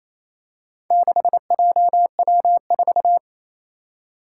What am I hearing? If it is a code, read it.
Morse code, heard as a keyed tone, about 27 wpm, 700 Hz. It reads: 6JW4
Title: Four Challenging Letter-Number Combinations 28wpm